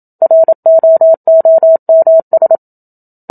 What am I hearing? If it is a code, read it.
Morse code, heard as a keyed tone, about 27 wpm, 650 Hz. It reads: ROOMS
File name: Book Two / 01-Chapter 1 - 27 WPM